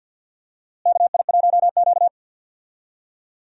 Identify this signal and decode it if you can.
Morse code, heard as a keyed tone, about 47 wpm, 700 Hz. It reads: KI1Q
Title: USA Call Signs - 4 Characters - Repeated 2x - 50wpm